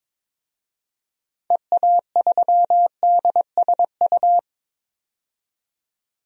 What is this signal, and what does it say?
Morse code, heard as a keyed tone, about 22 wpm, 700 Hz. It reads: EA3DSU
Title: DXCC-100 Call Signs - 6 Characters - 22wpm